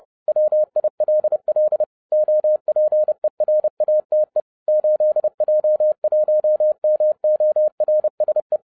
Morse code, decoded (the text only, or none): WILL OPERATE 8J1MORSE